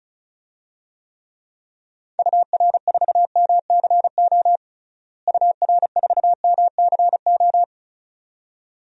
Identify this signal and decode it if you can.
Morse code, heard as a keyed tone, about 34 wpm, 700 Hz. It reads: UR4MCO UR4MCO
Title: DXCC-100 Call Signs - 6 Characters - Repeated 2x - 35wpm